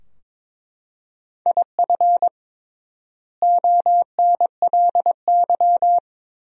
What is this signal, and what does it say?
Morse code, heard as a keyed tone, about 22 wpm, 700 Hz. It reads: IF ONLY
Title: 3x Word-Spacing - Sentences from Top 500 Words - 22wpm